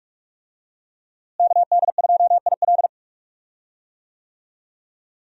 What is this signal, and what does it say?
Morse code, heard as a keyed tone, about 46 wpm, 700 Hz. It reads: KD2IL
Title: USA Call Signs - 5 Characters - Repeated 2x - 45wpm